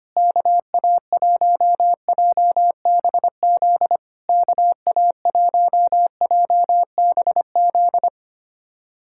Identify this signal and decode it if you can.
Morse code, heard as a keyed tone, about 25 wpm, 700 Hz. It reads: KA1JBZ KA1JBZ
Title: SOTA QSO Practice 25wpm